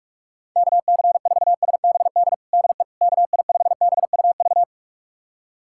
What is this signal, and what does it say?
Morse code, heard as a keyed tone, about 46 wpm, 700 Hz. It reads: KC4SBD DE KI5BUV
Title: USA Callsign DE Callsign - 6 Characters - 45wpm